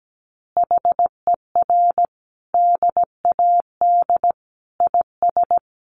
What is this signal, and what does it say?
Morse code, heard as a keyed tone, about 17 wpm, 700 Hz. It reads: HER DAD IS
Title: Additional Easy Phrases 17wpm